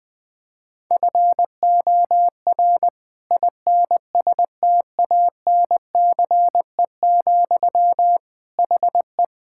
Morse code, heard as a keyed tone, about 20 wpm, 700 Hz. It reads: FOR INSTANCE, HE
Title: Sentences from Top 1000 Words - Review Part 5 - 20wpm